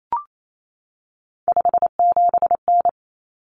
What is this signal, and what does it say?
Morse code, heard as a keyed tone, about 28 wpm, 700 Hz. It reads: 57N
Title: Common Cut Numbers - 3-Digit 28wpm